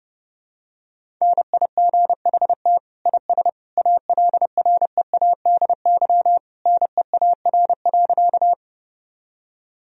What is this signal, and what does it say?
Morse code, heard as a keyed tone, about 32 wpm, 700 Hz. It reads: NIGHT IS ALREADY NEAR.
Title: Sentences from Top 1000 Words - Review Part 1 - 30wpm